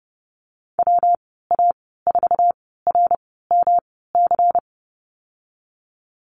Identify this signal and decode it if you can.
Morse code, heard as a keyed tone, about 30 wpm, spaced as for 17 wpm, 700 Hz. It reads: WA4RMC
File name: Lesson 37g - Letter x - Call Signs 30wpm